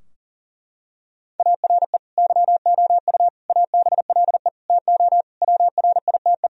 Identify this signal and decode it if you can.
Morse code, heard as a keyed tone, about 40 wpm, 700 Hz. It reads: ARE YOU ABLE TO WRITE
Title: Sentences from Top 1000 Words - Review Part 3 - 40wpm